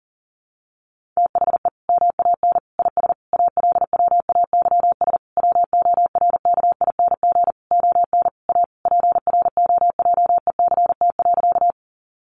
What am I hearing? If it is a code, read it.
Morse code, heard as a keyed tone, about 40 wpm, 700 Hz. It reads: THE MAN IS ALWAYS WORKING ON A PROJECT.
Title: Sentences from Top 1400 Words 40wpm